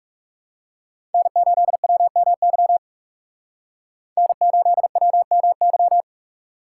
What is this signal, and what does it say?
Morse code, heard as a keyed tone, about 44 wpm, 700 Hz. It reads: N8WMY N8WMY